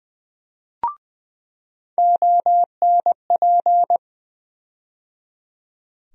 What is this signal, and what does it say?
Morse code, heard as a keyed tone, about 21 wpm, 700 Hz. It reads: ONP
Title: USA Call Sign Postfixes - 3 Letters 20wpm